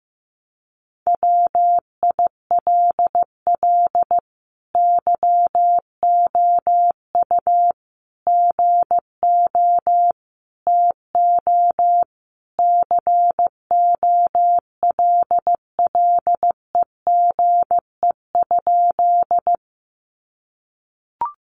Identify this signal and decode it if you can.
Morse code, heard as a keyed tone, about 15 wpm, 700 Hz. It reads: WILL YOU GO TO COLLEGE?
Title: Sentences from Top 1300 Words 15wpm